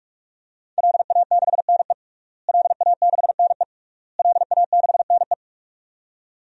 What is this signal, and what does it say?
Morse code, heard as a keyed tone, about 42 wpm, 700 Hz. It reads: LA6NE LA6NE LA6NE
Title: DXCC-100 Call Signs - 5 Characters - Repeated 3x - 45wpm